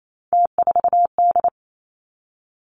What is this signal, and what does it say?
Morse code, heard as a keyed tone, about 28 wpm, 700 Hz. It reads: T4D
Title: Three Challenging Letter-Number Combinations 28wpm